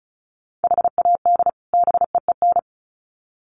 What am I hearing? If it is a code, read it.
Morse code, heard as a keyed tone, about 34 wpm, 700 Hz. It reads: HAD BEEN
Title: Sets of 2 Words 35wpm